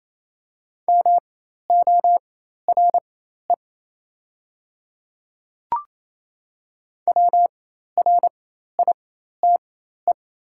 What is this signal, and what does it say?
Morse code, heard as a keyed tone, about 28 wpm, spaced as for 13 wpm, 700 Hz. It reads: MORE WRITE
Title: 4x Character Spacing - Top 100 Words - 28wpm